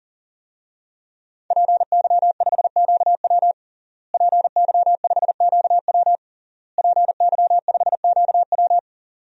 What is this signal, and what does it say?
Morse code, heard as a keyed tone, about 37 wpm, 700 Hz. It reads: PY5QW PY5QW PY5QW
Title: DXCC-100 Call Signs - 5 Characters - Repeated 3x - 40wpm